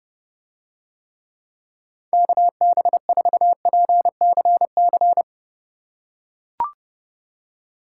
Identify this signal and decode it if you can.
Morse code, heard as a keyed tone, about 29 wpm, 700 Hz. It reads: KB4PCC